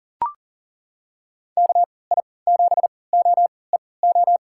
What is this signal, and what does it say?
Morse code, heard as a keyed tone, about 40 wpm, spaced as for 23 wpm, 700 Hz. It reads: KI7OEO
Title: USA Call Signs - 6 Characters - 3x Character Spacing - 40wpm